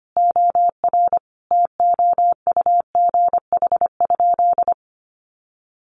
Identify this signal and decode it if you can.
Morse code, heard as a keyed tone, about 25 wpm, 700 Hz. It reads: OR TOUGH?
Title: Sentences from Top 1200 Words 25wpm